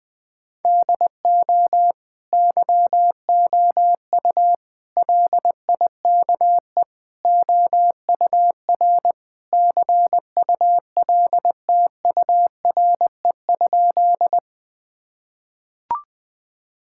Morse code, heard as a keyed tone, about 21 wpm, 700 Hz. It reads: DO YOU LIKE OUR CULTURE?
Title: Sentences from Top 1000 Words - Review Part 6 - 20wpm